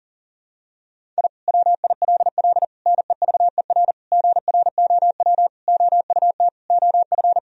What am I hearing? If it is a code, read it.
Morse code, heard as a keyed tone, about 37 wpm, 700 Hz. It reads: I WILL NEVER GROW OUT OF
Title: Sentences from Top 1000 Words - Review Part 1 - 40wpm